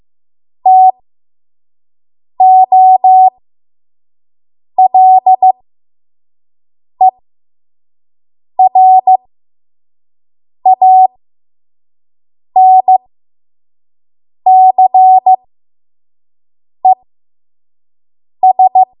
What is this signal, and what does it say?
Morse code, heard as a keyed tone, about 15 wpm, spaced as for 5 wpm, 750 Hz. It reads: TOLERANCES
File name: Morse / 05WPM